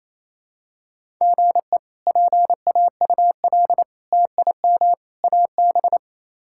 Morse code, heard as a keyed tone, about 27 wpm, 700 Hz. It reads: GE PAUL TIM AB
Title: K1USN SST QSO Practice - 28wpm